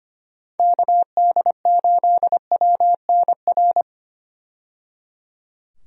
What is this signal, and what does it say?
Morse code, heard as a keyed tone, about 25 wpm, 700 Hz. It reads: KD8WNR
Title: USA Call Signs - 6 Characters - 25wpm